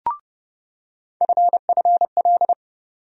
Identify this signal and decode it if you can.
Morse code, heard as a keyed tone, about 31 wpm, 700 Hz. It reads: FFL
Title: USA Call Sign Postfixes - 3 Letters 30wpm